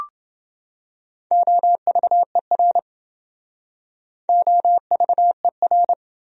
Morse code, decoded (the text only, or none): OVER OVER